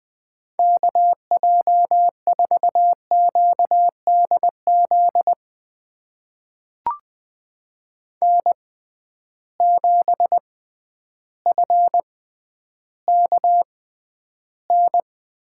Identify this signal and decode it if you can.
Morse code, heard as a keyed tone, about 20 wpm, spaced as for 11 wpm, 700 Hz. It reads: KJ4QDZ N7FKN
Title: USA Call Signs - 6x Character Spacing - 20wpm